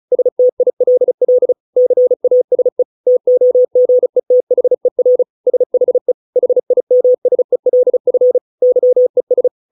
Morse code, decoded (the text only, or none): STILL CASE TOGETHER SHE HIMSELF YES